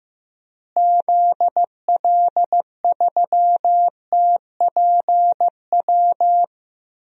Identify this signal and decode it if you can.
Morse code, heard as a keyed tone, about 15 wpm, 700 Hz. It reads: ZL3TPW
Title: DXCC-100 Call Signs - 6 Characters - Repeated 3x - 15wpm